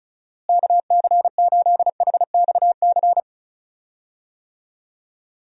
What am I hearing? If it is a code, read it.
Morse code, heard as a keyed tone, about 34 wpm, 700 Hz. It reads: KC8HXC
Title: USA Call Signs - 6 Characters - Repeated 2x - 35wpm